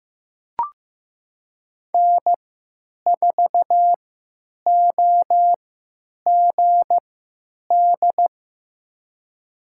Morse code, heard as a keyed tone, about 15 wpm, spaced as for 9 wpm, 700 Hz. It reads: N4OGD
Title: USA Call Signs - 5 Characters - 3x Character Spacing - 15wpm